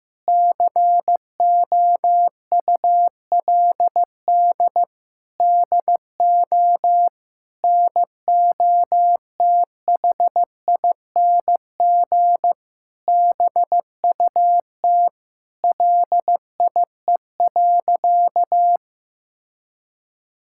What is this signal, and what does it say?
Morse code, heard as a keyed tone, about 15 wpm, 700 Hz. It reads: COULD DO NOTHING BUT LIE.
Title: Sentences from Top 1600 Words 15wpm